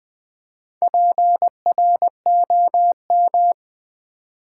Morse code, heard as a keyed tone, about 20 wpm, 700 Hz. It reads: PROM
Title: Build A Word - 2 Letters At A Time - 6-Letter Words - 20wpm